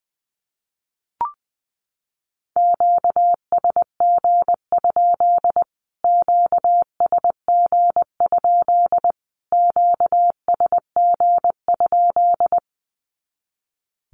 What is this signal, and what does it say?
Morse code, heard as a keyed tone, about 20 wpm, 700 Hz. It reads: QSG? QSG? QSG?
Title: Q-Codes Repeated 3x 20wpm